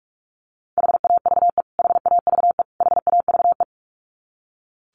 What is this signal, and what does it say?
Morse code, heard as a keyed tone, about 44 wpm, 700 Hz. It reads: HAVE HAVE HAVE
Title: Top 200 Words Repeated 45wpm